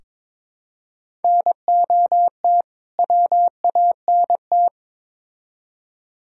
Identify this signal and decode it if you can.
Morse code, heard as a keyed tone, about 22 wpm, 700 Hz. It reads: NOT WANT
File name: practice / 22wpm / Build A Phrase - 5-Word Phrases - 22wpm